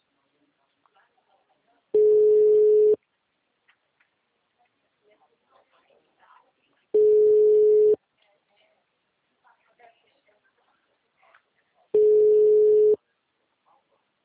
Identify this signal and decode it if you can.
Ringback tone (Europe, 425 Hz), with speech.